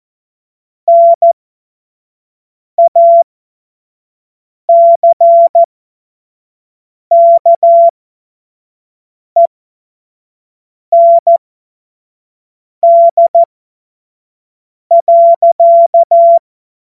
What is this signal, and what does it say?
Morse code, heard as a keyed tone, about 14 wpm, spaced as for 5 wpm, 680 Hz. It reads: NACKEND.